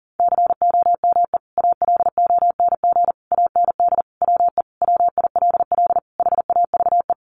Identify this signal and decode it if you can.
Morse code, heard as a keyed tone, about 40 wpm, 700 Hz. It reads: COME ALONG AND WE WILL HAVE
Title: Sentences from Top 1500 Words 40wpm